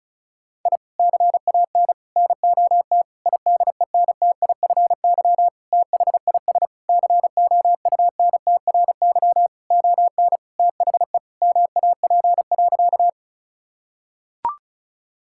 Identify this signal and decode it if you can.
Morse code, heard as a keyed tone, about 34 wpm, 700 Hz. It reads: I CAN NOT IDENTIFY THIS COUNTRY ON THE MAP.
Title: Sentences from Top 1000 Words - Review Part 6 - 35wpm